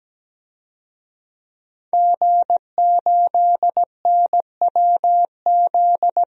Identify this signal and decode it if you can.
Morse code, heard as a keyed tone, about 17 wpm, 700 Hz. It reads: G8NWZ
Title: DXCC-100 Call Signs - 5 Characters - 17wpm